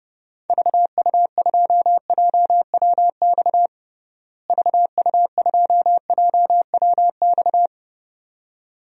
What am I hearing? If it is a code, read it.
Morse code, heard as a keyed tone, about 30 wpm, 700 Hz. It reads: VU2JWX VU2JWX
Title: DXCC-100 Call Signs - 6 Characters - Repeated 2x - 30wpm